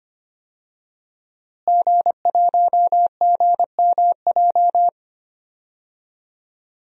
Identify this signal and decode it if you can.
Morse code, heard as a keyed tone, about 25 wpm, 700 Hz. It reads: G1GMJ
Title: DXCC-100 Call Signs - 5 Characters - Repeated 2x - 25wpm